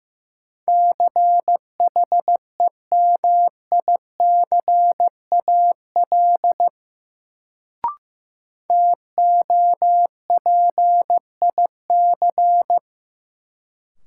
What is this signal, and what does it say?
Morse code, heard as a keyed tone, about 15 wpm, 700 Hz. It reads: CHEMICAL TOPIC
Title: Sentences from Top 1200 Words - New Word Drill Alt - 15wpm